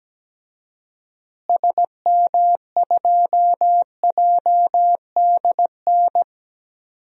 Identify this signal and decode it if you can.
Morse code, heard as a keyed tone, about 17 wpm, 700 Hz. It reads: SM2JDN
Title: DXCC-100 Call Signs - 6 Characters - Repeated 2x - 17wpm